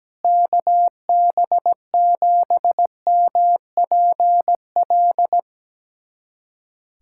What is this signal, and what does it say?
Morse code, heard as a keyed tone, about 17 wpm, 700 Hz. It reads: KB7MPL
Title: USA Callsign DE Callsign - 6 Characters - 17wpm